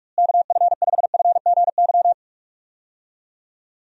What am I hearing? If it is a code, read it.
Morse code, heard as a keyed tone, about 46 wpm, 700 Hz. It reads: KF5FGY
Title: USA Call Signs - 6 Characters - Repeated 2x - 45wpm